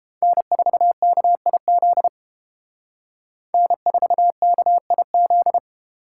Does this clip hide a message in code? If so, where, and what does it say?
Yes: Morse code from 0.2 to 5.6 s, heard as a keyed tone, about 28 wpm, 700 Hz. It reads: N4KIZ N4KIZ